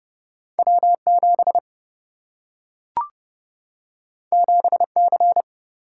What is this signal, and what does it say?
Morse code, heard as a keyed tone, about 30 wpm, 700 Hz. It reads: W 7 7 C
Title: Single-Letter Single-Number 30wpm